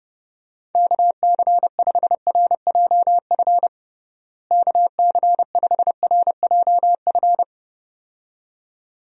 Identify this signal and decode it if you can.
Morse code, heard as a keyed tone, about 30 wpm, 700 Hz. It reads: KC5RJF KC5RJF
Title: USA Call Signs - 6 Characters - Repeated 2x - 30wpm